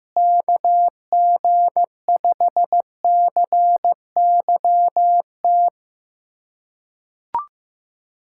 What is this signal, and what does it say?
Morse code, heard as a keyed tone, about 15 wpm, 700 Hz. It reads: KG5CYT
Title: USA Callsign DE Callsign - 6 Characters - 15wpm